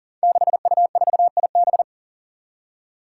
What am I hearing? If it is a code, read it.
Morse code, heard as a keyed tone, about 40 wpm, 700 Hz. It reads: 6U4IB
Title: Five Challenging Letter-Number Combinations 40wpm